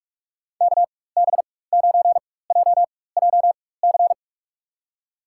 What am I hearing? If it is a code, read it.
Morse code, heard as a keyed tone, about 46 wpm, spaced as for 21 wpm, 700 Hz. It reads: KB9JJC
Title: USA Call Signs - 4x Character Spacing - 45wpm